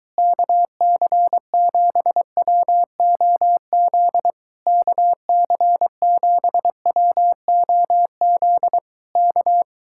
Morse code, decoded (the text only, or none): KC7WOZ KC7WOZ K